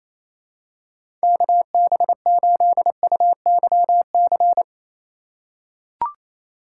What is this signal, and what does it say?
Morse code, heard as a keyed tone, about 28 wpm, 700 Hz. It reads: KB8UYC
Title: USA Call Signs - 4x Character Spacing - 28wpm